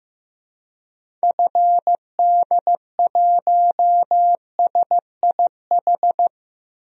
Morse code, heard as a keyed tone, about 15 wpm, 700 Hz. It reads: FD1SIH
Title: DXCC-100 Call Signs - 6 Characters - Repeated 3x - 15wpm